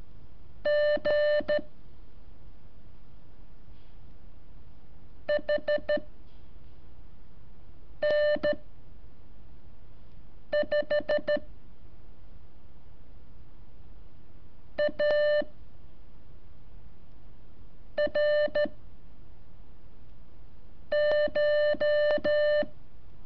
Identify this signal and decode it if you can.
Morse code, heard as a keyed tone, about 13 wpm, 590 Hz. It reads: G H N 5 A R CH
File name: curso / ejercicio 20